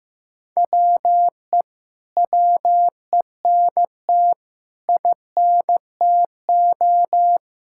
Morse code, heard as a keyed tone, about 15 wpm, 700 Hz. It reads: WE WENT INTO
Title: Sentences from Top 1600 Words 15wpm